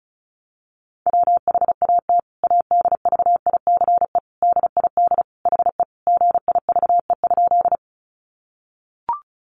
Morse code, heard as a keyed tone, about 34 wpm, 700 Hz. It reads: WHAT ADVICE DID HE GIVE?
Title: Sentences from Top 1100 Words 35wpm